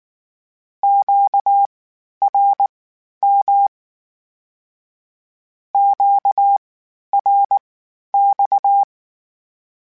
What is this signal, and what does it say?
Morse code, heard as a keyed tone, about 19 wpm, spaced as for 11 wpm, 800 Hz. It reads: QRM QRX
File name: Q Calls Twice sp @18